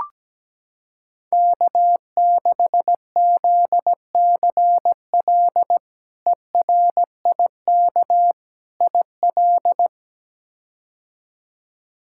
Morse code, heard as a keyed tone, about 17 wpm, 700 Hz. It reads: K6ZCL ERIK IL
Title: K1USN SST QSO Practice - 17wpm